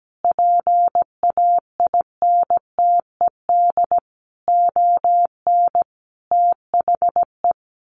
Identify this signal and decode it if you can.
Morse code, heard as a keyed tone, about 17 wpm, 700 Hz. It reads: PAINTED ON THE
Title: A Sound Of Thunder 17wpm